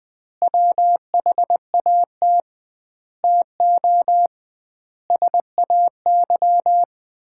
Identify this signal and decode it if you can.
Morse code, heard as a keyed tone, about 19 wpm, 700 Hz. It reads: WHAT TO SAY
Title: 2x Word-Spacing - Sets of 5 Words - 20wpm